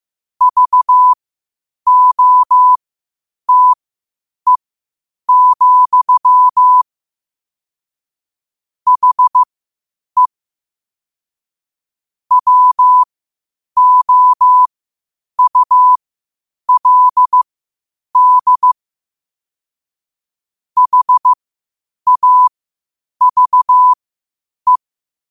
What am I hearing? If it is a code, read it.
Morse code, heard as a keyed tone, about 15 wpm, spaced as for 8 wpm, 1000 Hz. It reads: VOTE, HE WOULD HAVE
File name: qotdfiles / 8wpm